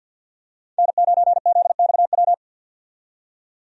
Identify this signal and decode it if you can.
Morse code, heard as a keyed tone, about 53 wpm, 700 Hz. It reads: N9ZXW
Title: USA Call Signs - 5 Characters - 50wpm